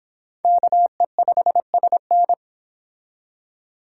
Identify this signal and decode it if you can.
Morse code, heard as a keyed tone, about 29 wpm, 700 Hz. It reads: KE5SN